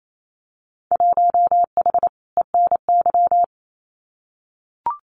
Morse code, heard as a keyed tone, about 28 wpm, 700 Hz. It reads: 1H ENY
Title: Winter Field Day Exchanges 28wpm